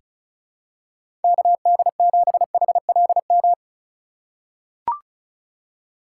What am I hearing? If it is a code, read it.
Morse code, heard as a keyed tone, about 36 wpm, 700 Hz. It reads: KD7HLM
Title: USA Call Signs - 6 Characters - 3x Character Spacing - 35wpm